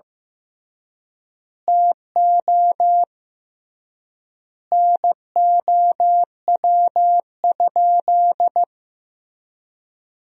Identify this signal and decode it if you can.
Morse code, heard as a keyed tone, about 15 wpm, 700 Hz. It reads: TO NOW?
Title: 3x Word-Spacing - Sentences from Top 200 Words - 15wpm